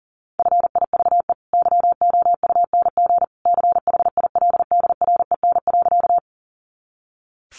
Morse code, heard as a keyed tone, about 40 wpm, 700 Hz. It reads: FIVE YOUNG CHILDREN.
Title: Sentences from Top 400 Words 40wpm